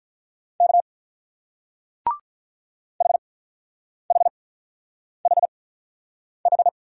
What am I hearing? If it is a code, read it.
Morse code, heard as a keyed tone, about 44 wpm, spaced as for 9 wpm, 690 Hz. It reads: K HHHH
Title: Speed-Racing Single Letters 35wpm